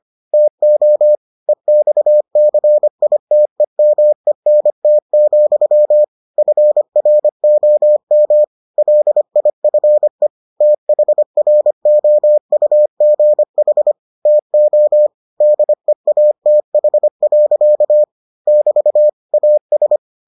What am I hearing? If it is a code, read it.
Morse code, heard as a keyed tone, about 25 wpm, 600 Hz. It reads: TO EXCITEMENT, FROM LIFE THROUGH TO DEATH. <BT> AS